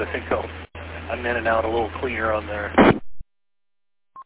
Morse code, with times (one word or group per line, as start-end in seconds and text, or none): none